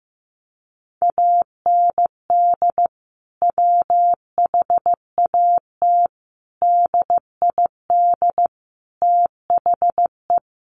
Morse code, heard as a keyed tone, about 15 wpm, 700 Hz. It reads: AND WHAT DID THE
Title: Sentences from Top 1500 Words 15wpm